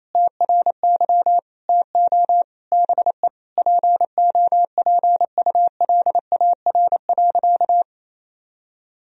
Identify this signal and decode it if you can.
Morse code, heard as a keyed tone, about 27 wpm, 700 Hz. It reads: TRY TO BE POPULAR.
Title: Sentences from Top 1000 Words - Review Part 3 - 28wpm